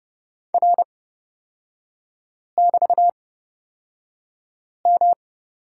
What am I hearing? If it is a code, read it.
Morse code, heard as a keyed tone, about 30 wpm, spaced as for 5 wpm, 700 Hz. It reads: R<BT>M
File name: Lesson 37e - Letter x - Three Characters 30wpm